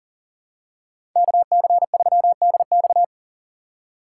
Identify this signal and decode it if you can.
Morse code, heard as a keyed tone, about 37 wpm, 700 Hz. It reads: KC3DX
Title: USA Call Signs - 5 Characters - 2x Character Spacing - 40wpm